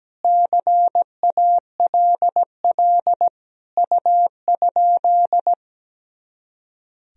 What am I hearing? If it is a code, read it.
Morse code, heard as a keyed tone, about 17 wpm, 700 Hz. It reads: CALL U?
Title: Sentences from Top 100 Words and Abbreviations 17wpm